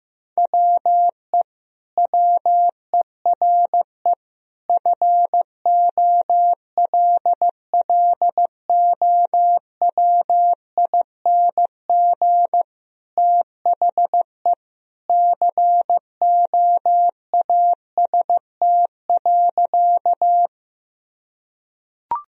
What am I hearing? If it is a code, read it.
Morse code, heard as a keyed tone, about 15 wpm, 700 Hz. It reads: WE WERE FOLLOWING THE COAST.
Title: Sentences from Top 1000 Words - Review Part 4 - 15wpm